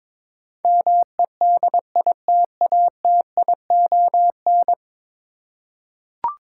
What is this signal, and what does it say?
Morse code, heard as a keyed tone, about 23 wpm, 700 Hz. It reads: MEDITATION
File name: practice / 22wpm / 10 Letter Words - First 3 Letters Given - 22wpm